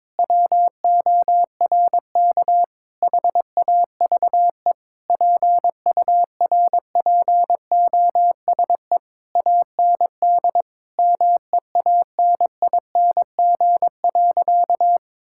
Morse code, heard as a keyed tone, about 22 wpm, 700 Hz. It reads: WORK HAVE PURPOSE AND MEANING.
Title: Sentences from Top 1000 Words - Review Part 6 - 22wpm